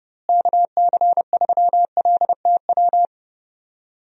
Morse code, heard as a keyed tone, about 30 wpm, 700 Hz. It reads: KC3LTW
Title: USA Call Signs - 6 Characters - Repeated 3x - 30wpm